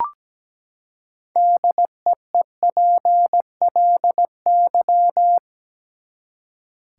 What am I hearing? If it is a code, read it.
Morse code, heard as a keyed tone, about 17 wpm, 700 Hz. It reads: DEEPLY
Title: Six Letter Words - 17wpm